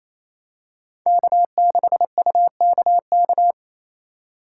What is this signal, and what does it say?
Morse code, heard as a keyed tone, about 27 wpm, 700 Hz. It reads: K6UKK
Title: USA Call Signs - 6x Character Spacing - 28wpm